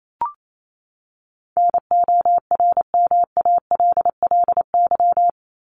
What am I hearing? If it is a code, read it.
Morse code, heard as a keyed tone, about 28 wpm, 700 Hz. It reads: NORMALLY
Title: LY Suffix - Top 100 Words with Base Words 28wpm